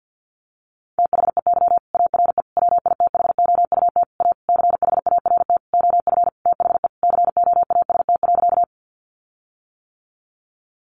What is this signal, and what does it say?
Morse code, heard as a keyed tone, about 50 wpm, 700 Hz. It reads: THEY ARE WITHOUT A CHART OF THE COAST.
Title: Sentences from Top 1300 Words 50wpm